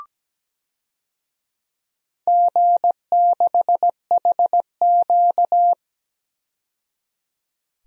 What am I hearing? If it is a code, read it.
Morse code, heard as a keyed tone, about 17 wpm, 700 Hz. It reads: G6HQ
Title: DXCC-100 Call Signs - 4 Characters - 17wpm